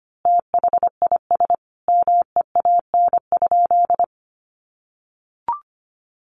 Morse code, heard as a keyed tone, about 25 wpm, 700 Hz. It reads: THIS MEAN?
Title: Sentences from Top 200 Words 25wpm